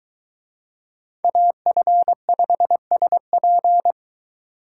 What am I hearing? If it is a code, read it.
Morse code, heard as a keyed tone, about 23 wpm, 700 Hz. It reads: AF5SP